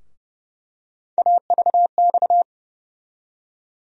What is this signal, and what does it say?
Morse code, heard as a keyed tone, about 32 wpm, 700 Hz. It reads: AVX
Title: USA Call Sign Postfixes - 3 Letters 30wpm